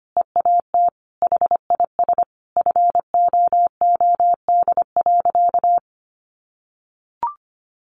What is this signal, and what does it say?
Morse code, heard as a keyed tone, about 25 wpm, 700 Hz. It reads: EAT HIS FOOD.
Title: Sentences from Top 1100 Words 25wpm